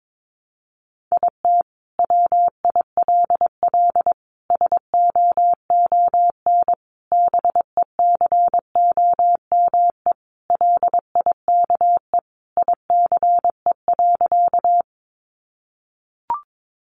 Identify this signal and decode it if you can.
Morse code, heard as a keyed tone, about 22 wpm, 700 Hz. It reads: IT WILL SOON BECOME LIKE ICE.
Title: Sentences from Top 1000 Words 22wpm